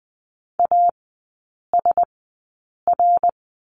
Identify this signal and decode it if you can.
Morse code, heard as a keyed tone, about 20 wpm, spaced as for 8 wpm, 700 Hz. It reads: ASR
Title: 067 Lesson 12e - Letter d - Three Characters 20wpm